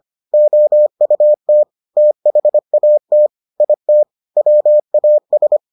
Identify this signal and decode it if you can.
Morse code, heard as a keyed tone, about 25 wpm, 600 Hz. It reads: OUT THAT IT WAS